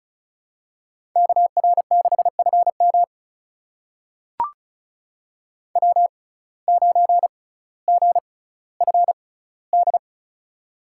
Morse code, heard as a keyed tone, about 36 wpm, spaced as for 18 wpm, 700 Hz. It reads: KR6FM W9GFD
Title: USA Call Signs - 5 Characters - 6x Character Spacing - 35wpm